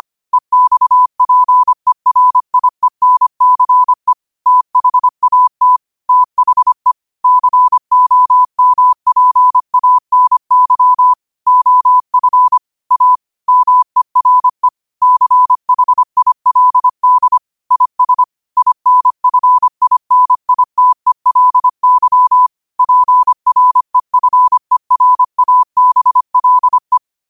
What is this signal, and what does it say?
Morse code, heard as a keyed tone, about 25 wpm, 1000 Hz. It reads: EXPERIENCE THAT THE COMPANY OF A MERE CHILD IS INFINITELY PREFERABLE
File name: qotdfiles / 25wpm